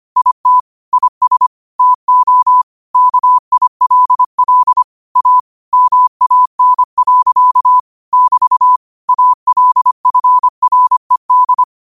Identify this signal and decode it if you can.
Morse code, heard as a keyed tone, about 25 wpm, 1000 Hz. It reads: IT IS TO KILL A MAN. <BT> ALFRED